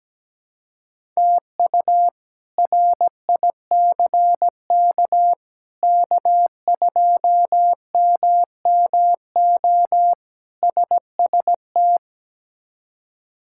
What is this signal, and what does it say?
Morse code, heard as a keyed tone, about 17 wpm, 700 Hz. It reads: TU RICK K2MMO SST
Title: K1USN SST QSO Practice - 17wpm